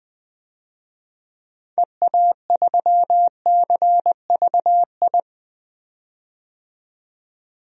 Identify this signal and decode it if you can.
Morse code, heard as a keyed tone, about 21 wpm, 700 Hz. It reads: EA3CVI
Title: DXCC-100 Call Signs - 6 Characters - 20wpm